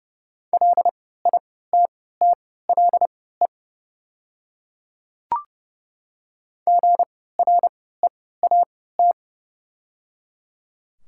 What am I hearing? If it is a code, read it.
Morse code, heard as a keyed tone, about 30 wpm, spaced as for 17 wpm, 690 Hz. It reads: LITTLE GREAT
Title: 3x Character Spacing - Top 200 Words - 30wpm